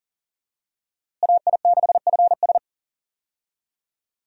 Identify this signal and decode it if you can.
Morse code, heard as a keyed tone, about 40 wpm, 700 Hz. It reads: AI6FS
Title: USA Call Signs - 5 Characters - Repeated 3x - 40wpm